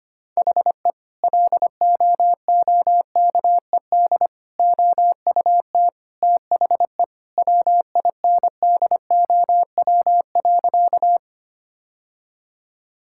Morse code, heard as a keyed tone, about 25 wpm, 700 Hz. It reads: HE LOOKED OUT THE WINDOW.
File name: practice / 25wpm / Sentences with 2-Word Phrases - 25wpm